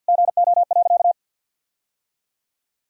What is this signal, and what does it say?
Morse code, heard as a keyed tone, about 50 wpm, 700 Hz. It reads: GO.
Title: Sentences from Top 1000 Words - Review Part 2 - 50wpm